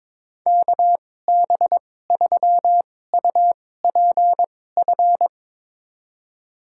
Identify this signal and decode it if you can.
Morse code, heard as a keyed tone, about 22 wpm, spaced as for 16 wpm, 700 Hz. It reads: KB3UPF
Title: USA Call Signs - 6 Characters - 2x Character Spacing - 22wpm